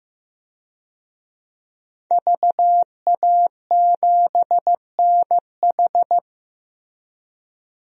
Morse code, heard as a keyed tone, about 15 wpm, 700 Hz. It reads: VA7NH
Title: DXCC-100 Call Signs - 5 Characters - 15wpm